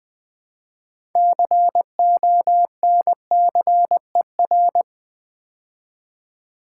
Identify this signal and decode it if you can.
Morse code, heard as a keyed tone, about 20 wpm, 700 Hz. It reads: CONCER
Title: Build A Word - 3 Letters At A Time - 9-Letter Words - 20wpm